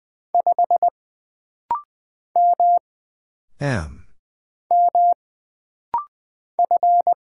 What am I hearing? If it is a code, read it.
Morse code, heard as a keyed tone, about 20 wpm, 700 Hz. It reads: 5 M M F
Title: Single Letter-Number Alt - Rapid-Fire 20wpm